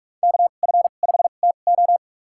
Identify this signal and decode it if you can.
Morse code, heard as a keyed tone, about 46 wpm, spaced as for 33 wpm, 700 Hz. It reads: KF5TO
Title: USA Call Signs - 5 Characters - 2x Character Spacing - 45wpm